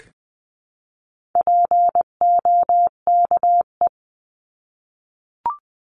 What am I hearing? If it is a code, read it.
Morse code, heard as a keyed tone, about 20 wpm, 700 Hz. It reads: POKE
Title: 259 Lesson 42d - Words 18-20wpm